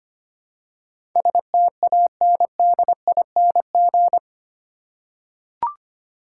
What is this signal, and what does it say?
Morse code, heard as a keyed tone, about 25 wpm, 700 Hz. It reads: STANDING
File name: 8 Letter Words - First 3 Letters Given - 25wpm